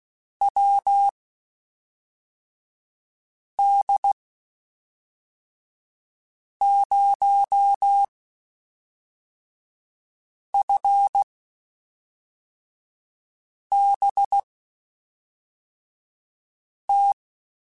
Morse code, heard as a keyed tone, about 16 wpm, spaced as for 3 wpm, 800 Hz. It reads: WD0FBT
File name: QSOs / QSOs-mp3s / WD0FBT 1@5